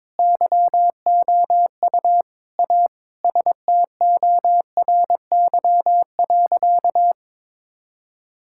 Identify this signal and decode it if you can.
Morse code, heard as a keyed tone, about 21 wpm, 700 Hz. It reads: YOU A STORY.
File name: practice / 22wpm / Sentences with 4-Word Phrases - 22wpm